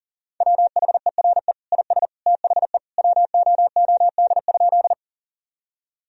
Morse code, heard as a keyed tone, about 40 wpm, 700 Hz. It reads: WHERE IS THE WOOD?
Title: Sentences from Top 1000 Words - Review Part 2 - 40wpm